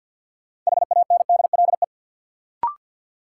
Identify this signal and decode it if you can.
Morse code, heard as a keyed tone, about 53 wpm, 700 Hz. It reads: HANDLE
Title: Sentences from Top 1100 Words - New Word Drill Alt - 50wpm